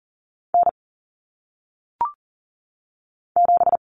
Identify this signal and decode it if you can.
Morse code, heard as a keyed tone, about 40 wpm, 700 Hz. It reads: N 7
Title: Common Cut Numbers - 1-Digit 40wpm